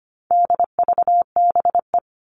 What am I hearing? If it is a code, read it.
Morse code, heard as a keyed tone, about 25 wpm, 700 Hz. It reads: DVBE
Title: All Cut Numbers - 4-Digit 25wpm